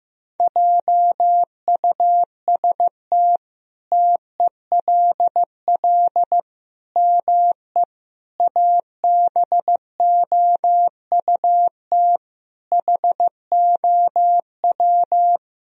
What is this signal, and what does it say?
Morse code, heard as a keyed tone, about 15 wpm, 700 Hz. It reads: JUST TELL ME ABOUT HOW
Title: Sentences from Top 1000 Words - Review Part 1 - 15wpm